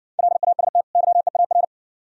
Morse code, heard as a keyed tone, about 60 wpm, 690 Hz. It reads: LAST YEAR